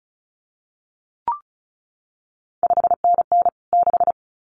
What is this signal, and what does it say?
Morse code, heard as a keyed tone, about 36 wpm, 700 Hz. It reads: 5NN 6
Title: CQ WW DX Contest Exchange Encore - 35wpm